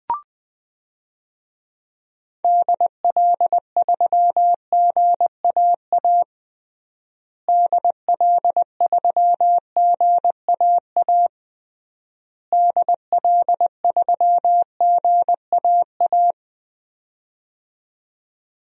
Morse code, heard as a keyed tone, about 21 wpm, 700 Hz. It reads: DL3GAA DL3GAA DL3GAA
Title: DXCC-100 Call Signs - 6 Characters - Repeated 3x - 20wpm